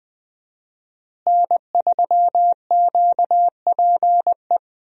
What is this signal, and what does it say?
Morse code, heard as a keyed tone, about 20 wpm, 700 Hz. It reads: N3QPE
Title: USA Call Signs - 5 Characters - Repeated 2x - 20wpm